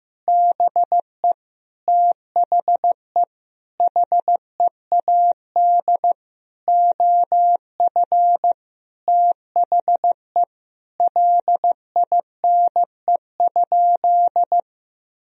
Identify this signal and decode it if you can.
Morse code, heard as a keyed tone, about 15 wpm, 700 Hz. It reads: BE THE HEAD OF THE LINE?
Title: Sentences from Top 1000 Words - Review Part 1 - 15wpm